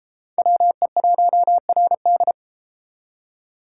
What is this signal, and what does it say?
Morse code, heard as a keyed tone, about 32 wpm, 700 Hz. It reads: WE1RD